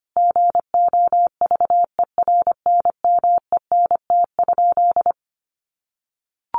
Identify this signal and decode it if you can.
Morse code, heard as a keyed tone, about 25 wpm, 700 Hz. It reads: GOVERNMENT?
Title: Sentences from Top 600 Words 25wpm